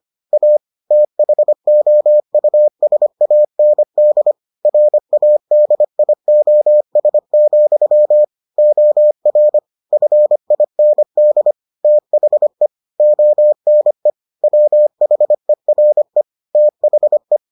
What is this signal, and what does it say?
Morse code, heard as a keyed tone, about 25 wpm, 600 Hz. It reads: A THOUSAND RADIOS, OR FIND THE ONE WHERE THE